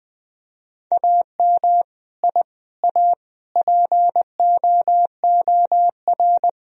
Morse code, heard as a keyed tone, about 20 wpm, 700 Hz. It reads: AM I A POOR
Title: Sentences from Top 1000 Words - Review Part 2 - 20wpm